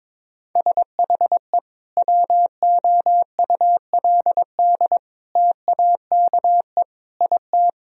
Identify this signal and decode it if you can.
Morse code, heard as a keyed tone, about 22 wpm, 700 Hz. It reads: SHE WOULD TAKE IT
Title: Sentences from Top 1000 Words - Review Part 1 - 22wpm